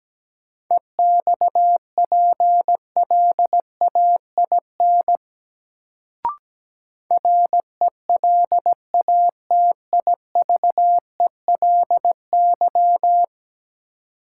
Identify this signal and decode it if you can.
Morse code, heard as a keyed tone, about 17 wpm, 700 Hz. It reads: EXPLAIN RELATIVELY
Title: Sentences from Top 1100 Words - New Word Drill - 17wpm